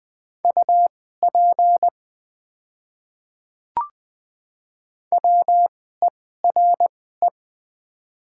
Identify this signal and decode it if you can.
Morse code, heard as a keyed tone, about 20 wpm, spaced as for 14 wpm, 700 Hz. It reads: UP WERE
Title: 2x Character Spacing - Top 200 Words - 20wpm